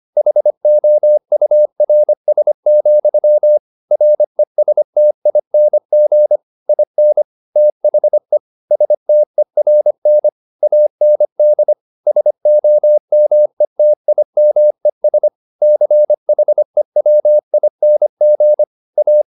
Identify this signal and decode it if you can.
Morse code, heard as a keyed tone, about 25 wpm, 600 Hz. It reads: HOURS, RESTING IN THE STERN AND SOMETIMES CHEWING A